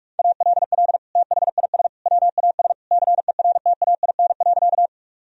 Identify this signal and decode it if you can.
Morse code, heard as a keyed tone, about 46 wpm, 700 Hz. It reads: ALL THIS WAS CERTAIN.
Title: Sentences from Top 1000 Words - Review Part 2 - 45wpm